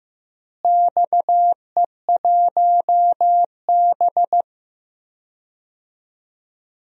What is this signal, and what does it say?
Morse code, heard as a keyed tone, about 15 wpm, 700 Hz. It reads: XE1B
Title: DXCC-100 Call Signs - 4 Characters - Repeated 2x - 15wpm